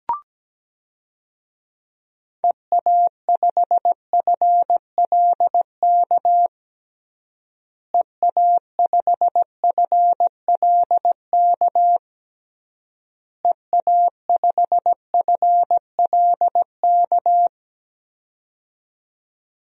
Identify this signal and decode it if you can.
Morse code, heard as a keyed tone, about 17 wpm, 700 Hz. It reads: EA5FLK EA5FLK EA5FLK
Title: DXCC-100 Call Signs - 6 Characters - Repeated 3x - 17wpm